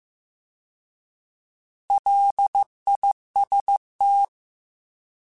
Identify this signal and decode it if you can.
Morse code, heard as a keyed tone, about 15 wpm, 800 Hz. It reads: LIST
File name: Alphabetical / Alphabetical mp3s / Ls @13